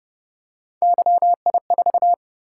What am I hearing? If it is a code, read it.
Morse code, heard as a keyed tone, about 29 wpm, 700 Hz. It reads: YI4